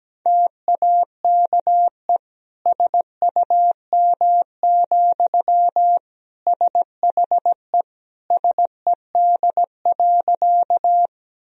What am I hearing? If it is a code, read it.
Morse code, heard as a keyed tone, about 17 wpm, 700 Hz. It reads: TAKE SUM, SHE SED.
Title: Sentences from Top 200 Words and Abbreviations 17wpm